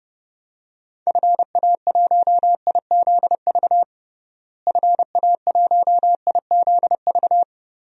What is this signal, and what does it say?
Morse code, heard as a keyed tone, about 30 wpm, 700 Hz. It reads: FA1IZV FA1IZV
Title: DXCC-100 Call Signs - 6 Characters - Repeated 3x - 30wpm